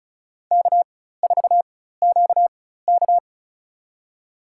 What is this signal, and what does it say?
Morse code, heard as a keyed tone, about 34 wpm, spaced as for 16 wpm, 700 Hz. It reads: K4QK
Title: USA Call Signs - 4 Characters - 4x Character Spacing - 35wpm